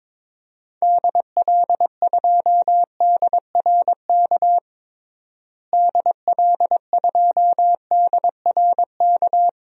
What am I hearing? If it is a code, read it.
Morse code, heard as a keyed tone, about 22 wpm, 700 Hz. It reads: DL2DRK DL2DRK
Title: DXCC-100 Call Signs - 6 Characters - Repeated 2x - 22wpm